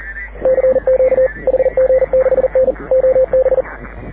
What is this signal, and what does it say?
Morse code, heard as a keyed tone, about 40 wpm, 520 Hz. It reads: CQ HG6N OB
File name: AUDIO2019 / HG6N CW SSB MIX